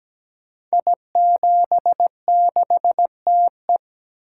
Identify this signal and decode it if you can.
Morse code, heard as a keyed tone, about 17 wpm, 700 Hz. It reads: I76TE
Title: Code Groups - 5 Letter:Numbers - 17wpm